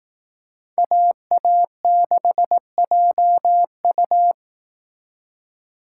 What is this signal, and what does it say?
Morse code, heard as a keyed tone, about 18 wpm, 700 Hz. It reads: AA6JU
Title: Speed Racing - USA Call Signs 15wpm